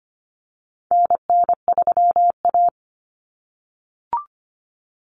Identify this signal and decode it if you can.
Morse code, heard as a keyed tone, about 25 wpm, 700 Hz. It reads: NN3A
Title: Common Cut Numbers - 4-Digit 25wpm